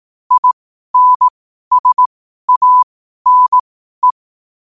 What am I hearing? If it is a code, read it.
Morse code, heard as a keyed tone, about 18 wpm, spaced as for 12 wpm, 1000 Hz. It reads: INSANE
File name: qotdfiles / 12wpm